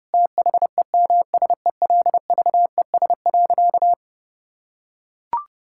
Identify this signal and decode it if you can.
Morse code, heard as a keyed tone, about 30 wpm, 700 Hz. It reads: THEMSELVES.
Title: Sentences from Top 1000 Words - Review Part 5 - 30wpm